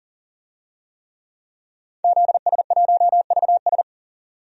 Morse code, heard as a keyed tone, about 40 wpm, 700 Hz. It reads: ZS1VS
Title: DXCC-100 Call Signs - 5 Characters - 40wpm